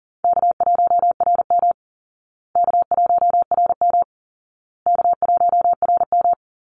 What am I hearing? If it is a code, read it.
Morse code, heard as a keyed tone, about 40 wpm, 700 Hz. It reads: K1RM K1RM K1RM
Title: Call Signs Repeated 2x - 40wpm